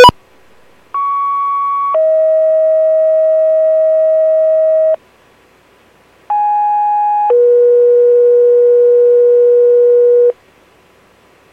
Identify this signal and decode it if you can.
Two-tone page, heard as a one-second tone, then a three-second tone, about 1120 then 640 Hz, about 830 then 480 Hz. It is heard from 0.9 to 10.3 s.